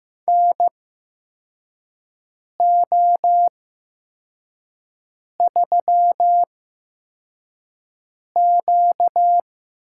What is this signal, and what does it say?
Morse code, heard as a keyed tone, about 15 wpm, spaced as for 4 wpm, 700 Hz. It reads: NO3Q
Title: USA Call Signs - 4 Characters - 8x Character Spacing - 15wpm